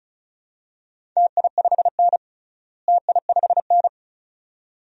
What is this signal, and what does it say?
Morse code, heard as a keyed tone, about 36 wpm, 700 Hz. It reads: TI5N TI5N
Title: DXCC-100 Call Signs - 4 Characters - Repeated 2x - 35wpm